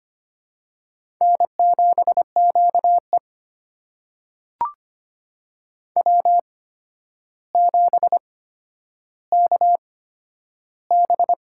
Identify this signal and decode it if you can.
Morse code, heard as a keyed tone, about 25 wpm, spaced as for 11 wpm, 700 Hz. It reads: N7QE W7KB
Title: USA Call Signs - 4 Characters - 8x Character Spacing - 25wpm